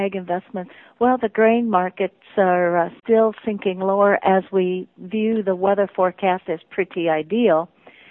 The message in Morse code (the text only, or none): none